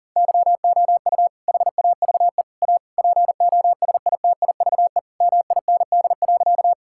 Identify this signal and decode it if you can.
Morse code, heard as a keyed tone, about 40 wpm, 700 Hz. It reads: YOU HAVE A POSITIVE MIND.
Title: Sentences from Top 1000 Words - Review Part 5 - 40wpm